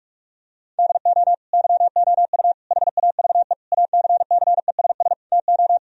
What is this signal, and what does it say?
Morse code, heard as a keyed tone, about 48 wpm, 700 Hz. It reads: DO YOU HAVE ACCESS TO
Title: Sentences from Top 1000 Words - Review Part 4 - 45wpm